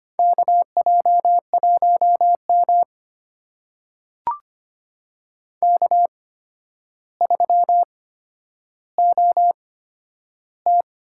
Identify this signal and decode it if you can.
Morse code, heard as a keyed tone, about 25 wpm, spaced as for 11 wpm, 700 Hz. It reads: KJ1M K3OT
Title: USA Call Signs - 4 Characters - 8x Character Spacing - 25wpm